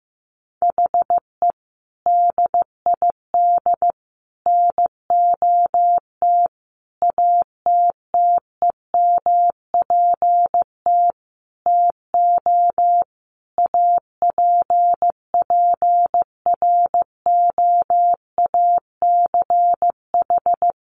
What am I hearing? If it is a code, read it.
Morse code, heard as a keyed tone, about 15 wpm, 700 Hz. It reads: HE DID NOT ATTEMPT TO APPROACH
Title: Sentences from Top 1400 Words 15wpm